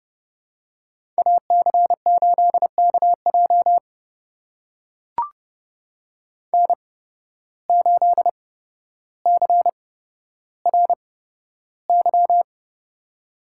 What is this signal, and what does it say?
Morse code, heard as a keyed tone, about 30 wpm, spaced as for 13 wpm, 700 Hz. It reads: AC8KJ N8CRY
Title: USA Call Signs - 8x Character Spacing - 30wpm